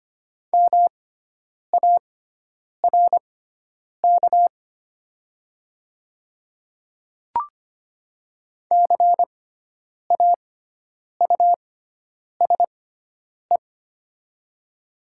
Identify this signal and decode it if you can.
Morse code, heard as a keyed tone, about 25 wpm, spaced as for 9 wpm, 700 Hz. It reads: MARK CAUSE
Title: 6x Character Spacing - Top 300 Words - 25wpm